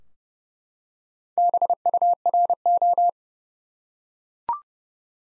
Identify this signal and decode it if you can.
Morse code, heard as a keyed tone, about 30 wpm, 700 Hz. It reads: BURO
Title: Top 90 Abbr 30WPM